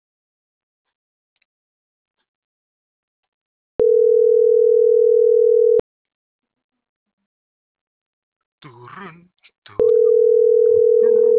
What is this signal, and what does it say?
Ringback tone (North America), 440 + 480 Hz.